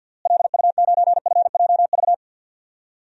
Morse code, heard as a keyed tone, about 46 wpm, 700 Hz. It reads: LU9FJV
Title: DXCC-100 Call Signs - 6 Characters - Repeated 3x - 50wpm